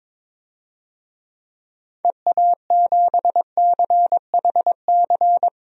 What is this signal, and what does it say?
Morse code, heard as a keyed tone, about 22 wpm, 700 Hz. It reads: EA7CHC
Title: DXCC-100 Call Signs - 6 Characters - Repeated 3x - 22wpm